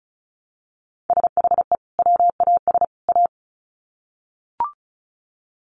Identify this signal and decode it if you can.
Morse code, heard as a keyed tone, about 34 wpm, 700 Hz. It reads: SHE WAS A
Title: Sets of 3 Words 35wpm